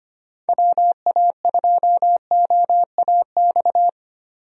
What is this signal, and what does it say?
Morse code, heard as a keyed tone, about 25 wpm, 700 Hz. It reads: WA2OAX
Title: USA Call Signs - 6 Characters - 25wpm